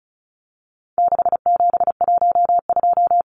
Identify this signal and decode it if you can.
Morse code, heard as a keyed tone, about 34 wpm, 700 Hz. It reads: 6712
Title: Numbers - 4-Digit 35wpm